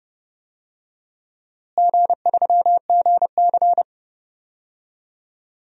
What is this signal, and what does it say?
Morse code, heard as a keyed tone, about 31 wpm, 700 Hz. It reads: G3GC
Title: DXCC-100 Call Signs - 4 Characters - Repeated 2x - 30wpm